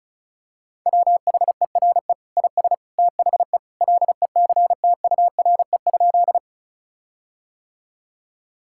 Morse code, heard as a keyed tone, about 36 wpm, 700 Hz. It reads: WHERE IS THE LECTURE?
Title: Sentences from Top 1600 Words 35wpm